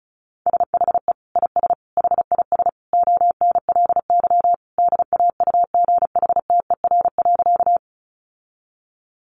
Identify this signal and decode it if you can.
Morse code, heard as a keyed tone, about 34 wpm, 700 Hz. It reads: SHE IS HIS ONLY DAUGHTER.
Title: Sentences from Top 1500 Words 35wpm